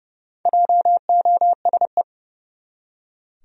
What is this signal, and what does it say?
Morse code, heard as a keyed tone, about 30 wpm, 700 Hz. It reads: JOSE
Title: USA Common Names - Max 4 Letters - 30wpm